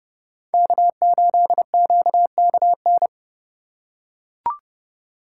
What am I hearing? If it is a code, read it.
Morse code, heard as a keyed tone, about 29 wpm, 700 Hz. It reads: K8QKN